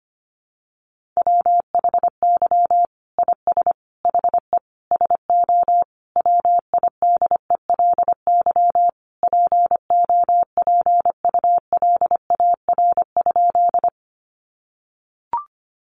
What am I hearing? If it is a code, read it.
Morse code, heard as a keyed tone, about 25 wpm, 700 Hz. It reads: WHY IS HE SO WIDELY POPULAR?
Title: Sentences from Top 1200 Words 25wpm